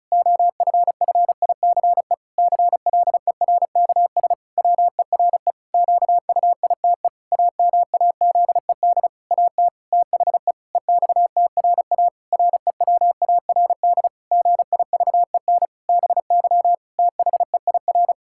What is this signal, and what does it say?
Morse code, heard as a keyed tone, about 36 wpm, 700 Hz. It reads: OFFICE CLERKS WERE QUITE AMAZED AT THE EXTRA REWARD GIVEN BY THEIR
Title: Pangram Challenge v2 - 35wpm